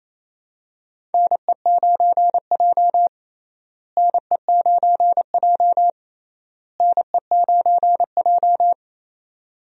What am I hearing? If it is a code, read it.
Morse code, heard as a keyed tone, about 28 wpm, 700 Hz. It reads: NE9J NE9J NE9J
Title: USA Call Signs - 4 Characters - Repeated 3x - 28wpm